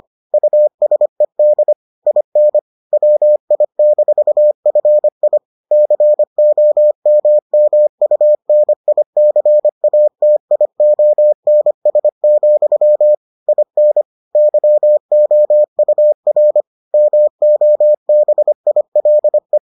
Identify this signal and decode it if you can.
Morse code, heard as a keyed tone, about 25 wpm, 600 Hz. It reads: USED IN WI-FI COMMUNICATIONS, IN YOUR MOBILE